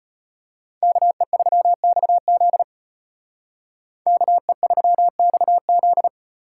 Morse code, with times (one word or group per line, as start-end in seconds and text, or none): 0.8-2.6 KE3XZ
4.1-6.1 KE3XZ